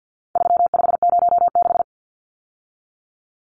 Morse code, heard as a keyed tone, about 48 wpm, 700 Hz. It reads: 3506
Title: Numbers - 4-Digit 50wpm